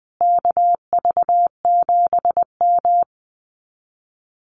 Morse code, heard as a keyed tone, about 20 wpm, 700 Hz. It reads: KV7M
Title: Common QSO Phrases Encore 20wpm